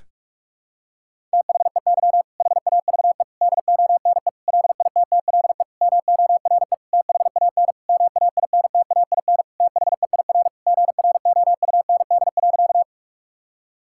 Morse code, heard as a keyed tone, about 45 wpm, 700 Hz. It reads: THEY HAVE DONE LITTLE MORE THAN MAINTAIN THEIR GROUND.
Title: Sentences from Top 1000 Words - Review Part 5 - 45wpm